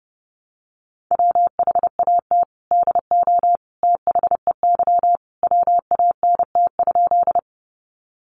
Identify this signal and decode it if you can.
Morse code, heard as a keyed tone, about 28 wpm, 700 Hz. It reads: WHAT DO THEY WANT?
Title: Sentences from Top 200 Words 30wpm